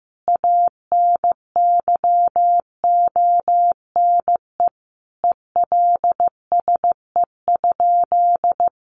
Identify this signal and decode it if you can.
Morse code, heard as a keyed tone, about 15 wpm, 700 Hz. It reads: ANYONE ELSE?
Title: Sentences from Top 900 Words 15wpm